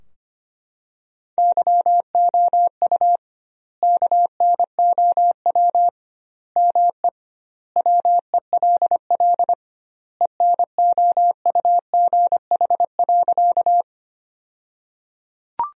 Morse code, heard as a keyed tone, about 25 wpm, 700 Hz. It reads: YOU KNOW ME WELL ENOUGH.
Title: 2x Word-Spacing - Sentences from Top 400 Words - 25wpm